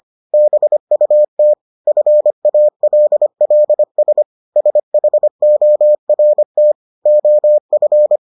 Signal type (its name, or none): Morse code